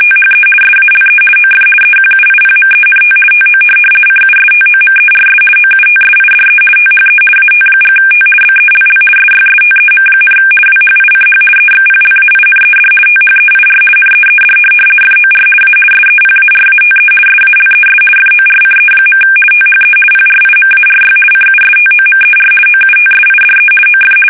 RTTY, about 50 baud. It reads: THE SALE OF EQUIPMENT TO CUSTOMERS OUTSIDE THE AT&T BELL SYSTEM, WHICH EXPLAINED THEIR NEED FOR A SEPARATE SALES FORCE. THE PRIMARY CUSTOMER OUTSIDE OF
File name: fsk850-50B / teletype-corp